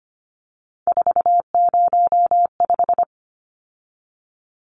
Morse code, heard as a keyed tone, about 25 wpm, 700 Hz. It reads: 405
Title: Numbers - 3-Digit 25wpm